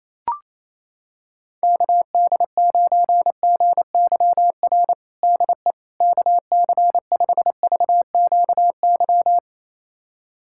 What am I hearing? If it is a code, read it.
Morse code, heard as a keyed tone, about 27 wpm, 700 Hz. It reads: KD9GYR DE KC5VQY
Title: USA Callsign DE Callsign - 6 Characters - 28wpm